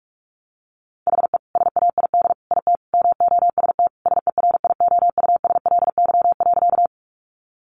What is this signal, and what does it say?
Morse code, heard as a keyed tone, about 44 wpm, 700 Hz. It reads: HE SAID IT MOST SERIOUSLY.
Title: Sentences from Top 1200 Words 45wpm